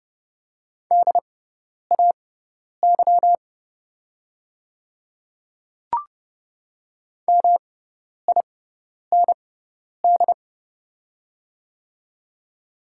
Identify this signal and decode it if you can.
Morse code, heard as a keyed tone, about 30 wpm, spaced as for 10 wpm, 700 Hz. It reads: DAY MIND
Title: 6x Character Spacing - Top 500 Words - 30wpm